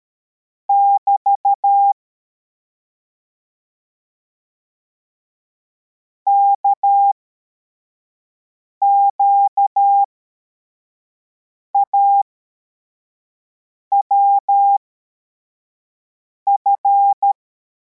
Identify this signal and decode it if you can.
Morse code, heard as a keyed tone, about 13 wpm, spaced as for 4 wpm, 800 Hz. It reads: <BT> KQAWF
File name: formation / cw / Text71